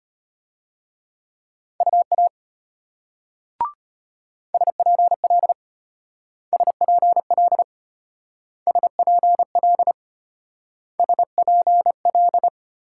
Morse code, heard as a keyed tone, about 33 wpm, 700 Hz. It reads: UA SPL SPL SPL SPL